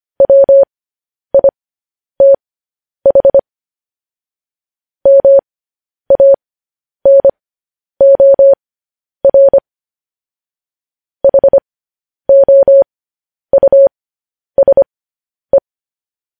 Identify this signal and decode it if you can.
Morse code, heard as a keyed tone, about 25 wpm, spaced as for 10 wpm, 550 Hz. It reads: WITH MANOR HOUSE
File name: podcast / News headlines-1x-25x10